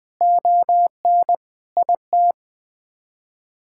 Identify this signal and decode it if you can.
Morse code, heard as a keyed tone, about 21 wpm, 700 Hz. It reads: ON IT
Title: Sets of 9 Words - 20wpm